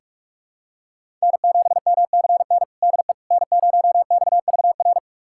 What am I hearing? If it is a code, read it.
Morse code, heard as a keyed tone, about 41 wpm, 700 Hz. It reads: N7MCN DE N0XVR
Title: USA Callsign DE Callsign - 5 Characters - 45wpm